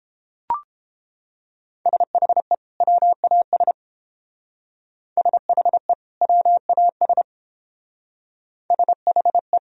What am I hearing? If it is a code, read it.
Morse code, heard as a keyed tone, about 31 wpm, 700 Hz. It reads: SHE WAS SHE WAS SHE